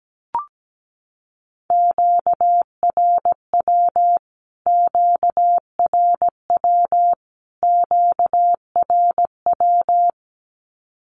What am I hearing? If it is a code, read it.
Morse code, heard as a keyed tone, about 17 wpm, 700 Hz. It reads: QRW QRW QRW
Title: Q-Codes Repeated 3x 17wpm